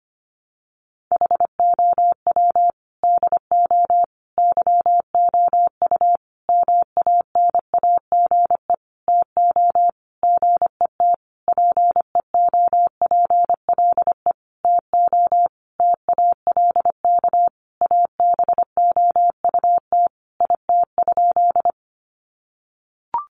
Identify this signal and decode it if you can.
Morse code, heard as a keyed tone, about 25 wpm, 700 Hz. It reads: HOW DO YOU MANAGE TO GET PEOPLE TO TALK ABOUT IT?
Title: Sentences from Top 1400 Words 25wpm